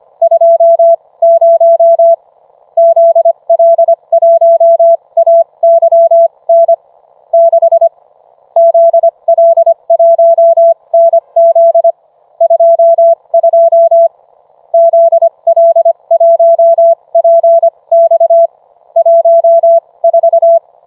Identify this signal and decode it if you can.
Morse code, heard as a keyed tone, about 25 wpm, spaced as for 21 wpm, 670 Hz. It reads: 20 ZL1AYN 6 ZL1NZ 22 ZL1PX 14